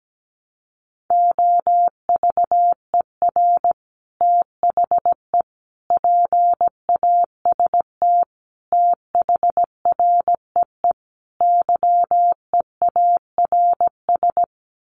Morse code, heard as a keyed tone, about 17 wpm, 700 Hz. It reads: OVER THE PAST THREE YEARS
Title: Sets of 5 Words Encore2 17wpm